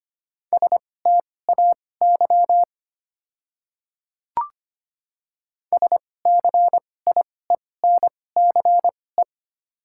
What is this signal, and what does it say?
Morse code, heard as a keyed tone, about 25 wpm, spaced as for 18 wpm, 690 Hz. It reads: STAY SCIENCE
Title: 2x Character Spacing - Top 500 Words - 25wpm